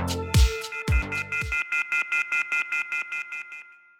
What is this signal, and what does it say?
Off-hook warning tone, 1400 + 2060 + 2450 + 2600 Hz, with noise.